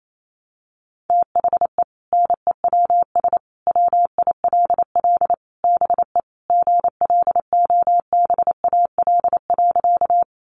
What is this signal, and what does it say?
Morse code, heard as a keyed tone, about 28 wpm, 700 Hz. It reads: THE NEWS WILL BE GLOBAL.
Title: Sentences from Top 1200 Words 28wpm